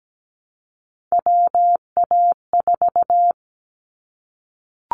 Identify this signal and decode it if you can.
Morse code, heard as a keyed tone, about 17 wpm, 700 Hz. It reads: WA4
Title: Callsign Prefixes for Top 25 DXCC - Repeated 3x - 17wpm